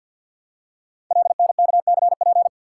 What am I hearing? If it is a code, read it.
Morse code, heard as a keyed tone, about 52 wpm, 700 Hz. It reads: LNKCP
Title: Code Groups - 5 Letter:Numbers - 50wpm